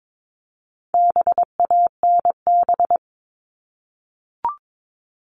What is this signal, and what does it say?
Morse code, heard as a keyed tone, about 22 wpm, 700 Hz. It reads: BANB
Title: All Cut Numbers - 4-Digit 22wpm